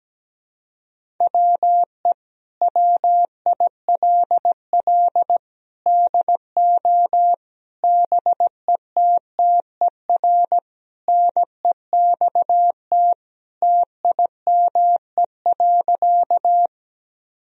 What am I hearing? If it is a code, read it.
Morse code, heard as a keyed tone, about 17 wpm, 700 Hz. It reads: WE WILL DO BETTER NEXT TIME.
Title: Sentences from Top 1000 Words - Review Part 1 - 17wpm